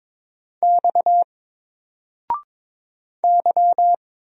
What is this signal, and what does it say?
Morse code, heard as a keyed tone, about 22 wpm, 700 Hz. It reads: X Y
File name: Single-Letter-Number-22wpm